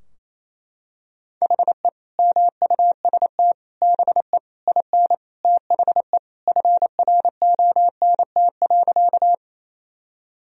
Morse code, heard as a keyed tone, about 28 wpm, 700 Hz. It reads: HE MUST BE IN THE FRONT.
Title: Sentences from Top 1000 Words - Review Part 2 - 28wpm